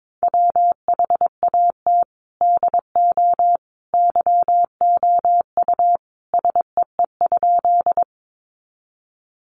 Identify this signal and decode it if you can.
Morse code, heard as a keyed tone, about 22 wpm, 700 Hz. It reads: WHAT DO YOU SEE?
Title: Sentences from Top 100 Words 22wpm